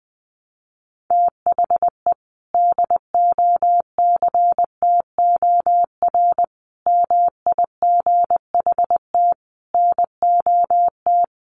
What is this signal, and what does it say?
Morse code, heard as a keyed tone, about 20 wpm, 700 Hz. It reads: THE DOCTOR MIGHT NOT
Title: Sentences from Top 1500 Words 20wpm